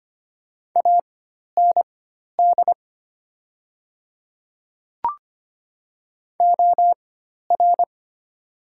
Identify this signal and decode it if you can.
Morse code, heard as a keyed tone, about 25 wpm, spaced as for 12 wpm, 700 Hz. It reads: AND OR
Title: 4x Character Spacing - Top 100 Words - 25wpm